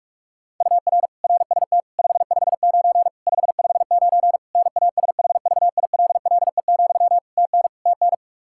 Morse code, heard as a keyed tone, about 47 wpm, 700 Hz. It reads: UR RST 559 559 NASHVILLE, TN TN
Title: Practice QSO - Exchange 1 - Beginning - 45wpm